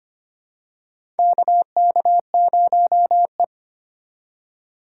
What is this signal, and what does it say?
Morse code, heard as a keyed tone, about 25 wpm, 700 Hz. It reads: KK0E
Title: USA Call Signs - 4 Characters - 4x Character Spacing - 25wpm